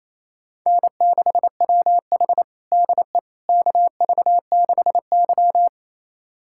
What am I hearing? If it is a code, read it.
Morse code, heard as a keyed tone, about 27 wpm, 690 Hz. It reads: N6WH DE KV6Y
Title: USA Callsign DE Callsign - 4 Characters - 28wpm